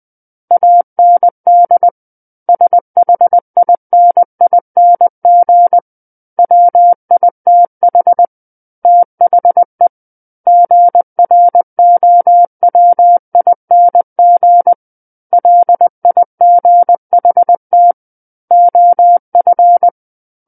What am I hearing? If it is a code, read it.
Morse code, heard as a keyed tone, about 20 wpm, 700 Hz. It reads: AND SHINING WITH THE GROWING LIGHT OF
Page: SKCC Learning Center - Morse Code Practice Files